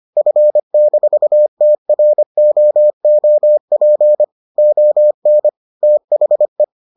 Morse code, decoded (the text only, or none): F-TROOP ON THE